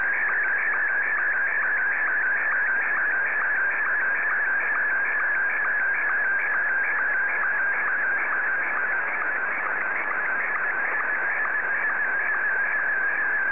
SSTV (slow-scan television), Martin 1.
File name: sstv / M1-ZL2TJM